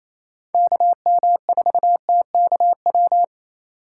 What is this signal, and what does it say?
Morse code, heard as a keyed tone, about 36 wpm, 700 Hz. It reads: KM4TKW